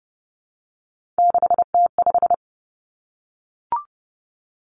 Morse code, heard as a keyed tone, about 30 wpm, 700 Hz. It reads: 6T5
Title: Three Challenging Letter-Number Combinations 30wpm